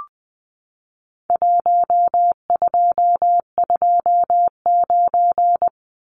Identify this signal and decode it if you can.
Morse code, heard as a keyed tone, about 20 wpm, 700 Hz. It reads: 1229
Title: Numbers - Contest Style - 4-Digit 20wpm